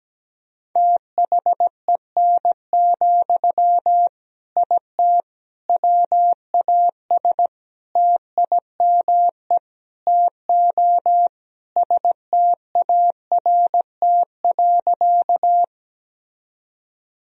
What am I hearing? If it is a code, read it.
Morse code, heard as a keyed tone, about 17 wpm, 700 Hz. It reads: THEN, IT WAS TIME TO START.
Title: Sentences from Top 1000 Words - Review Part 1 - 17wpm